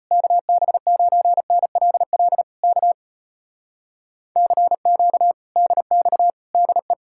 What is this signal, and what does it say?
Morse code, heard as a keyed tone, about 33 wpm, 700 Hz. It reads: KB9NLL K CQ DX DE